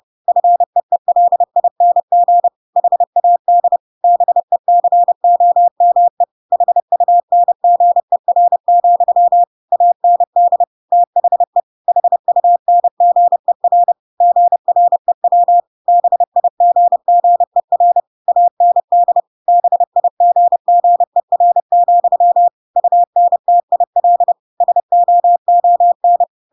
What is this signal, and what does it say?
Morse code, heard as a keyed tone, about 30 wpm, 700 Hz. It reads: FEELING HAD BECOME HUNGER, AND THE HUNGER GREW BIGGER AND BIGGER, UNTIL SOON